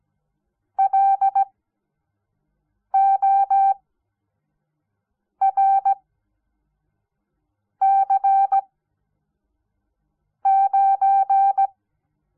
Morse code, heard as a keyed tone, about 17 wpm, spaced as for 5 wpm, 780 Hz. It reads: LORC9
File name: Onlinelehrgang / morsen / DJ4UF Morsen06